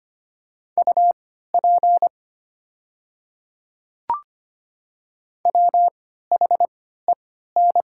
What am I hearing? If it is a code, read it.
Morse code, heard as a keyed tone, about 25 wpm, spaced as for 14 wpm, 700 Hz. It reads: UP WHEN
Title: 3x Character Spacing - Top 100 Words - 25wpm